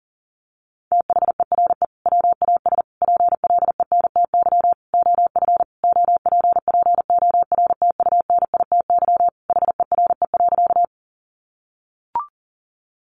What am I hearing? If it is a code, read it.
Morse code, heard as a keyed tone, about 40 wpm, 700 Hz. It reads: THERE WAS PLENTY OF OPPORTUNITY HERE.
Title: Sentences from Top 1300 Words 40wpm